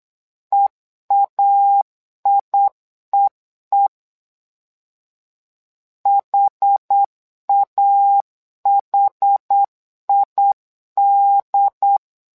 Morse code, heard as a keyed tone, about 8 wpm, 800 Hz. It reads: EAIEE HAHID
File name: LEZIONI / 2 UHDN / 2 UHD 8wpm 2258